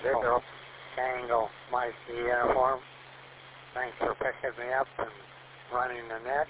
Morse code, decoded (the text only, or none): none